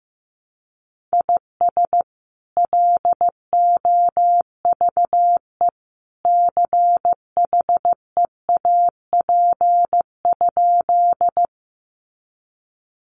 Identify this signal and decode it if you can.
Morse code, heard as a keyed tone, about 15 wpm, 700 Hz. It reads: IS LOVE CHEAP?
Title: Sentences from Top 1200 Words 15wpm